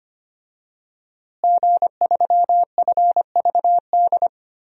Morse code, heard as a keyed tone, about 25 wpm, 700 Hz. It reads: G3FVD
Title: DXCC-100 Call Signs - 5 Characters - Repeated 3x - 25wpm